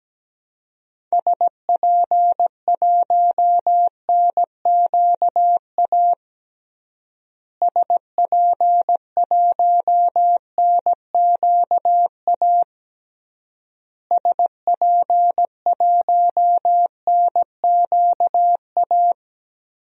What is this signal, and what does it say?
Morse code, heard as a keyed tone, about 17 wpm, 700 Hz. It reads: SP1NQA SP1NQA SP1NQA
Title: DXCC-100 Call Signs - 6 Characters - Repeated 3x - 17wpm